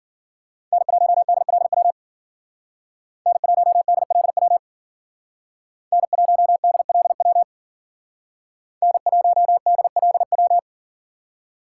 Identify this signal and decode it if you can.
Morse code, heard as a keyed tone, about 52 wpm, 700 Hz. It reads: N1DLW N1DLW N1DLW N1DLW